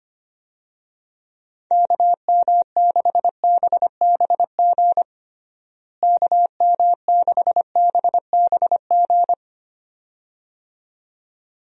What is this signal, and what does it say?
Morse code, heard as a keyed tone, about 25 wpm, 700 Hz. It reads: KM6BBG KM6BBG
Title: USA Call Signs - 6 Characters - Repeated 2x - 25wpm